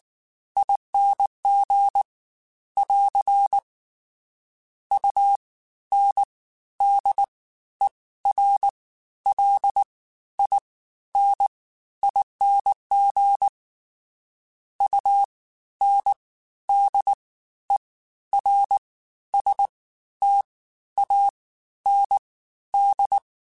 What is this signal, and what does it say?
Morse code, heard as a keyed tone, about 19 wpm, spaced as for 12 wpm, 800 Hz. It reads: ING <AR> UNDERLINING UNDERSTAND
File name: Special INGs-mp3s / Special INGs-mp3s / Fast ing @18